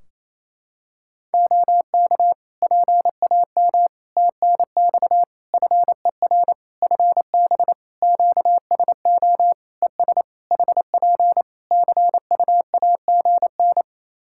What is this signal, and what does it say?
Morse code, heard as a keyed tone, about 28 wpm, 700 Hz. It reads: OK PAM TNX FER FB QSO ES HP CUAGN
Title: K1USN SST QSO Practice 28wpm